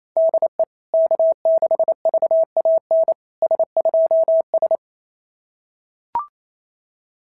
Morse code, heard as a keyed tone, about 28 wpm, 660 Hz. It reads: DE K6VAN S2S
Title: SOTA QSO Practice 28wpm